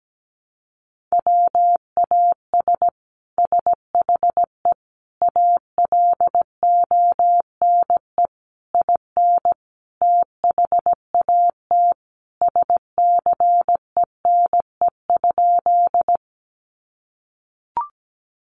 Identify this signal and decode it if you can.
Morse code, heard as a keyed tone, about 17 wpm, 700 Hz. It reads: WAS SHE ALONE IN THAT SCENE?
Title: Sentences from Top 1100 Words 17wpm